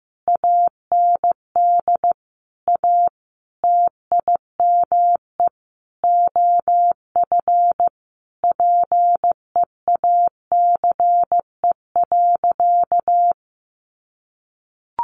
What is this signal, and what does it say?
Morse code, heard as a keyed tone, about 15 wpm, 700 Hz. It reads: AND A TIME OF PEACE.
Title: Sentences from Top 1500 Words 15wpm